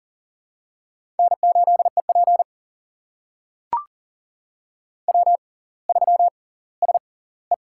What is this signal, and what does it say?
Morse code, heard as a keyed tone, about 40 wpm, spaced as for 21 wpm, 700 Hz. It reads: N8EP W3SE
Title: USA Call Signs - 4 Characters - 6x Character Spacing - 40wpm